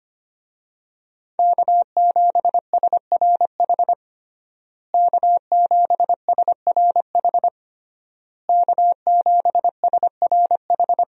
Morse code, heard as a keyed tone, about 25 wpm, 700 Hz. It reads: K7SRH K7SRH K7SRH
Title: USA Call Signs - 5 Characters - Repeated 3x - 25wpm